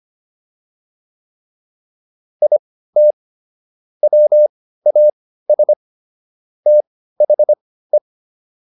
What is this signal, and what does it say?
Morse code, heard as a keyed tone, about 25 wpm, spaced as for 15 wpm, 600 Hz. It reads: IT WAS THE
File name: WD205 15